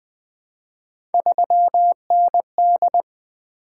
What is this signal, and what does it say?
Morse code, heard as a keyed tone, about 20 wpm, 700 Hz. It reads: 3ND
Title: Three Letter-Number Combinations - 20wpm